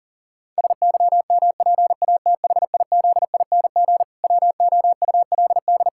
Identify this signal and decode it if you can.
Morse code, heard as a keyed tone, about 40 wpm, 700 Hz. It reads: SYMPATHIZING WOULD
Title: Pangram Challenge v2 - 40wpm